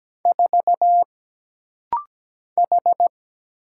Morse code, heard as a keyed tone, about 17 wpm, 700 Hz. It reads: 4 H
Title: Single Letter-Number Alt - Rapid-Fire 17wpm